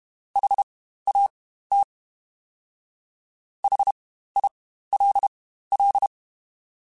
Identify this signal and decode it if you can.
Morse code, heard as a keyed tone, about 32 wpm, spaced as for 15 wpm, 800 Hz. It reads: HAT HILL
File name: Spaced Alphas / Spaced Alphas mp3s / Hs sp @30